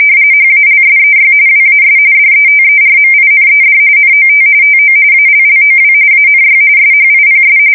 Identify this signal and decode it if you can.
RTTY, about 45.45 baud. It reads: HALSEY WAS DETERMINED NOT TO ALLOW ANY SHIPS
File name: fsk170 / USS-New-Jersey